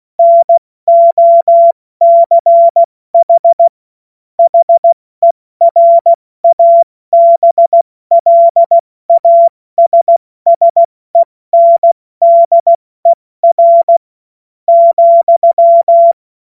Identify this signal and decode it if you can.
Morse code, heard as a keyed tone, about 16 wpm, spaced as for 14 wpm, 680 Hz. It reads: NOCH HERABLASSENDER,
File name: Morsekurs / Witze / Witz 31. August 2021, 70 F80